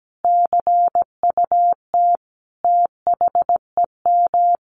Morse code, heard as a keyed tone, about 17 wpm, 700 Hz. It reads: CUT THEM
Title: Sentences from Top 1000 Words 17wpm